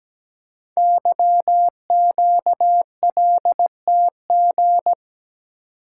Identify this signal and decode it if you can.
Morse code, heard as a keyed tone, about 17 wpm, 700 Hz. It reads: YQLTG
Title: Code Groups - 5 Letters - 17wpm